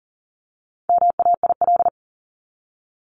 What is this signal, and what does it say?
Morse code, heard as a keyed tone, about 40 wpm, 700 Hz. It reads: MAIL
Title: Top 1751-2000 Words 40wpm